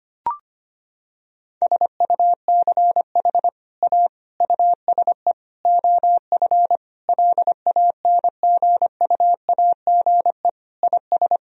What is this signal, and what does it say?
Morse code, heard as a keyed tone, about 25 wpm, 700 Hz. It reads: SUCH A USE OF LANGUAGE IS
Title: Sentences from Top 1000 Words - Review Part 3 - 25wpm